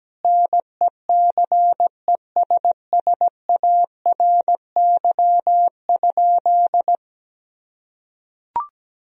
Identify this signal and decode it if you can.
Morse code, heard as a keyed tone, about 17 wpm, 700 Hz. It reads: NECESSARY?
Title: Sentences from Top 1000 Words - Review Part 3 - 17wpm